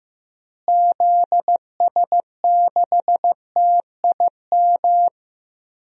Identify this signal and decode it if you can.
Morse code, heard as a keyed tone, about 15 wpm, 700 Hz. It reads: ZS6TIM
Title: DXCC-100 Call Signs - 6 Characters - Repeated 2x - 15wpm